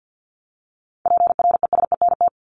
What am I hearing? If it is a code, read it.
Morse code, heard as a keyed tone, about 50 wpm, 700 Hz. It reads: PRESENT
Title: ED Suffix - Top 100 Words with Base Words 50wpm